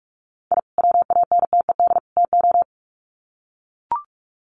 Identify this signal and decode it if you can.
Morse code, heard as a keyed tone, about 44 wpm, 700 Hz. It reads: I WANTED TO
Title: Sets of 3 Words 45wpm